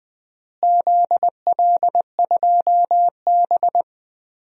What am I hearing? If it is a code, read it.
Morse code, heard as a keyed tone, about 20 wpm, 700 Hz. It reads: ZL2B
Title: DXCC-100 Call Signs - 4 Characters - Repeated 2x - 20wpm